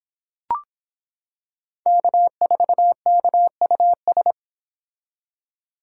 Morse code, heard as a keyed tone, about 26 wpm, 700 Hz. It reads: K4KUS